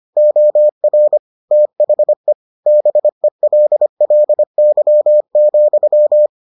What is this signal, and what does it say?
Morse code, heard as a keyed tone, about 25 wpm, 600 Hz. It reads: OR THE BELLY,